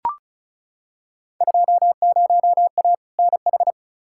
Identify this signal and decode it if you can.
Morse code, heard as a keyed tone, about 36 wpm, 700 Hz. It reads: 20A NH
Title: Field Day Exchanges Encore - 35wpm